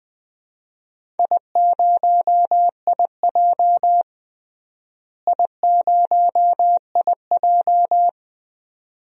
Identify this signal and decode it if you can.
Morse code, heard as a keyed tone, about 20 wpm, 700 Hz. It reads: I0IJ I0IJ
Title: DXCC-100 Call Signs - 4 Characters - Repeated 2x - 20wpm